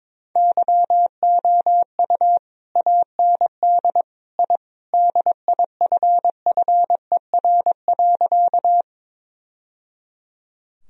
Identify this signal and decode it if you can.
Morse code, heard as a keyed tone, about 22 wpm, 700 Hz. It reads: YOU AND I DIFFER.
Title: Sentences from Top 1000 Words - Review Part 1 - 22wpm